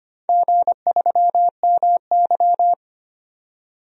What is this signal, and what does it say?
Morse code, heard as a keyed tone, about 25 wpm, 700 Hz. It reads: G3MY
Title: DXCC-100 Call Signs - 4 Characters - 25wpm